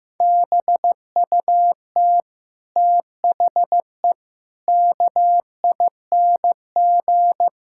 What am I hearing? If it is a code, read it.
Morse code, heard as a keyed tone, about 16 wpm, 700 Hz. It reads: BUT THE KING
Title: Sentences from Top 1600 Words 15wpm